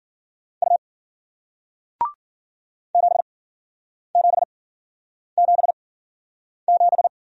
Morse code, heard as a keyed tone, about 53 wpm, spaced as for 9 wpm, 700 Hz. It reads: U 7777